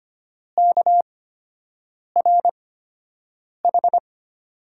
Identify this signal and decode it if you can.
Morse code, heard as a keyed tone, about 25 wpm, spaced as for 7 wpm, 700 Hz. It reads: KRH
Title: USA Call Signs - 8x Character Spacing - 25wpm